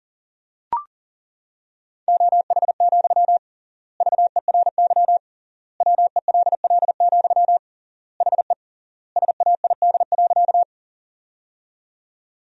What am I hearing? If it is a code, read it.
Morse code, heard as a keyed tone, about 40 wpm, 700 Hz. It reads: OH, VERY WELL, HE SAID.
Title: 3x Word-Spacing - Sentences from Top 500 Words - 40wpm